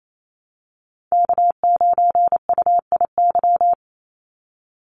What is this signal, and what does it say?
Morse code, heard as a keyed tone, about 28 wpm, 700 Hz. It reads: K9UIY
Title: Call Signs Repeated 3x - 28wpm